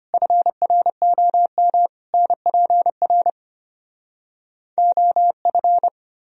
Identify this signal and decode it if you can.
Morse code, heard as a keyed tone, about 33 wpm, 700 Hz. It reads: FROM NPR OF